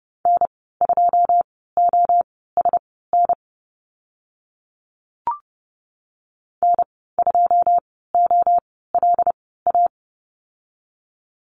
Morse code, heard as a keyed tone, about 30 wpm, spaced as for 17 wpm, 700 Hz. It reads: N2OSN N2OLA
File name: Lesson 15g - Number 5 - Call Signs 30wpm